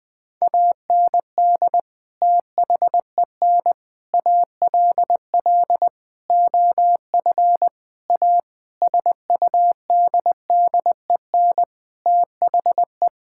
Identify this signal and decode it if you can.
Morse code, heard as a keyed tone, about 20 wpm, 700 Hz. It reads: AND THEN ALL OF A SUDDEN THE
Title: Sentences with 6-Word Phrases - 20wpm